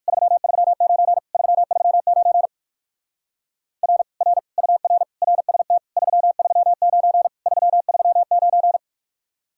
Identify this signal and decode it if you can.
Morse code, heard as a keyed tone, about 54 wpm, 700 Hz. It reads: 339 339 R R UR RST 339 339
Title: Speed-Racing Common QSO Phrases - 45wpm